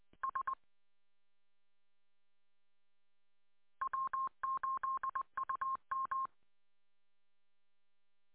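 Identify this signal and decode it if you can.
Morse code, heard as a keyed tone, about 20 wpm, 1070 Hz. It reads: S W8UM
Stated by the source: Morse on 145.230 MHz.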